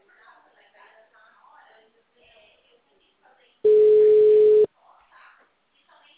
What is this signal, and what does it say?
Ringback tone (Europe, 425 Hz), with speech.
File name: out- -2006- - - .10093